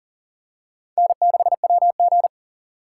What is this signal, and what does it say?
Morse code, heard as a keyed tone, about 40 wpm, 700 Hz. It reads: N6WG
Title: USA Call Signs - 4 Characters - 3x Character Spacing - 40wpm